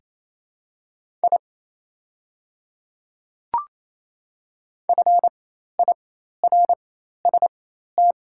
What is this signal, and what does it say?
Morse code, heard as a keyed tone, about 28 wpm, spaced as for 13 wpm, 700 Hz. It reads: I FIRST
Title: 4x Character Spacing - Top 100 Words - 28wpm